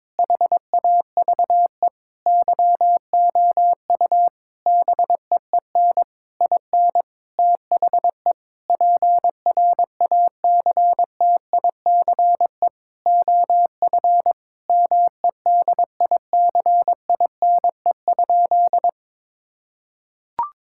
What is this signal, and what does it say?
Morse code, heard as a keyed tone, about 22 wpm, 700 Hz. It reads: HAVE YOU BEEN IN THE PRACTICE OF MEDICINE?
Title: Sentences from Top 1000 Words - Review Part 6 - 22wpm